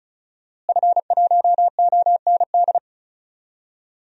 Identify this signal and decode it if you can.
Morse code, heard as a keyed tone, about 36 wpm, 700 Hz. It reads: F1OND
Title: DXCC-100 Call Signs - 5 Characters - Repeated 2x - 35wpm